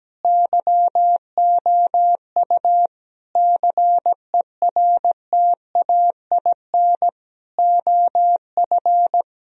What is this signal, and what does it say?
Morse code, heard as a keyed tone, about 17 wpm, 700 Hz. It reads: YOU CERTAIN OF
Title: Sentences from Top 1000 Words - Review Part 3 - 17wpm